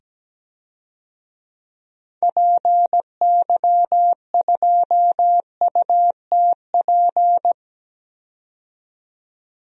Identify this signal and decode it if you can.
Morse code, heard as a keyed tone, about 17 wpm, 700 Hz. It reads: PY2UTP
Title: DXCC-100 Call Signs - 6 Characters - 17wpm